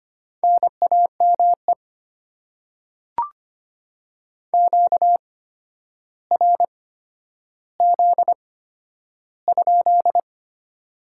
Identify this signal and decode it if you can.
Morse code, heard as a keyed tone, about 25 wpm, spaced as for 11 wpm, 700 Hz. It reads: NAME QRZ?
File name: Top 90 QSO Elements - 8x Character Spacing - 25wpm